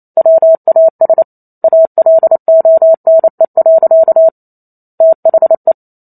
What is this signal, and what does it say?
Morse code, heard as a keyed tone, about 28 wpm, 650 Hz. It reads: WAS ALONE. THE